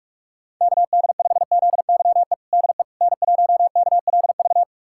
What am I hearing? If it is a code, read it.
Morse code, heard as a keyed tone, about 43 wpm, 700 Hz. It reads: KD5ZYE DE N1KLV
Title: USA Callsign DE Callsign 45wpm